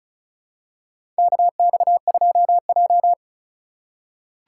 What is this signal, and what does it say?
Morse code, heard as a keyed tone, about 36 wpm, 700 Hz. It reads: KX2J
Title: USA Call Signs - 4 Characters - 4x Character Spacing - 35wpm